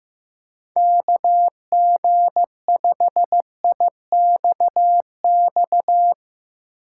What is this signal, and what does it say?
Morse code, heard as a keyed tone, about 15 wpm, 700 Hz. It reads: KG5IXX
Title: USA Call Signs - 6 Characters - 4x Character Spacing - 15wpm